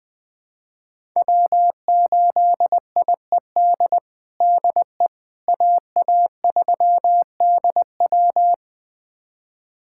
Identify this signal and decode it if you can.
Morse code, heard as a keyed tone, about 21 wpm, 700 Hz. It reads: W8IED DE AA3DW
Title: USA Callsign DE Callsign - 5 Characters - 20wpm